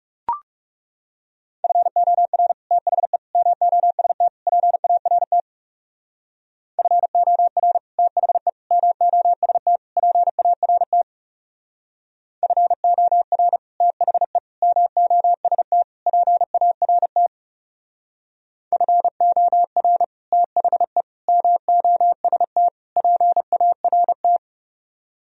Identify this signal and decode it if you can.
Morse code, heard as a keyed tone, about 37 wpm, 700 Hz. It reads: FOR THE MOST PART FOR THE MOST PART FOR THE MOST PART FOR THE MOST PART
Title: Speed-Racing - Sets of 4 Words 30wpm